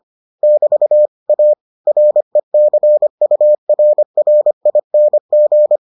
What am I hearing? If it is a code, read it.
Morse code, heard as a keyed tone, about 25 wpm, 600 Hz. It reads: <BT> A RECURRING